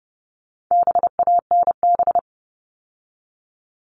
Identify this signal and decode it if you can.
Morse code, heard as a keyed tone, about 30 wpm, 700 Hz. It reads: BANB
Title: All Cut Numbers - 4-Digit 30wpm